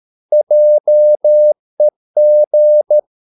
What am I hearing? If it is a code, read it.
Morse code, heard as a keyed tone, about 13 wpm, 600 Hz. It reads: JEG
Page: Ferie 13 wpm